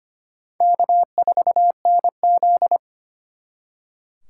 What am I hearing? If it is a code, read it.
Morse code, heard as a keyed tone, about 31 wpm, 700 Hz. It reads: K4NZ
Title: Speed Racing - USA 1x2 Call Signs 25wpm